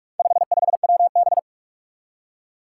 Morse code, heard as a keyed tone, about 46 wpm, 700 Hz. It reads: 55WB
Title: Four Challenging Letter-Number Combinations 45wpm